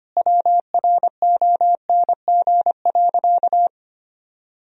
Morse code, heard as a keyed tone, about 25 wpm, 700 Hz. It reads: WRONG.
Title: Sentences from Top 1000 Words - Review Part 3 - 25wpm